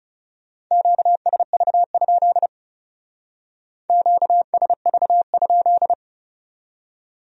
Morse code, heard as a keyed tone, about 37 wpm, 700 Hz. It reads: QSV? QSV?
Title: Speed-Racing Q-Codes - 30wpm